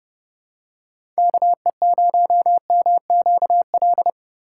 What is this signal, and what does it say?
Morse code, heard as a keyed tone, about 30 wpm, 700 Hz. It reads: KE0MQL
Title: USA Call Signs - 6 Characters - 2x Character Spacing - 30wpm